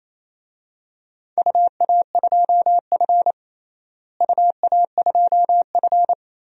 Morse code, heard as a keyed tone, about 27 wpm, 700 Hz. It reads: UA2F UA2F
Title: DXCC-100 Call Signs - 4 Characters - Repeated 3x - 28wpm